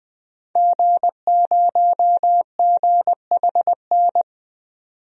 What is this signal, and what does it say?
Morse code, heard as a keyed tone, about 20 wpm, 700 Hz. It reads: G0GHN
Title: DXCC-100 Call Signs - 5 Characters - 20wpm